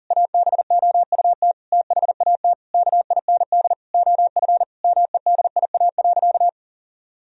Morse code, heard as a keyed tone, about 40 wpm, 700 Hz. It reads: ABOUT THAT KIND OF MEDIA.
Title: Sentences from Top 1000 Words - Review Part 4 - 40wpm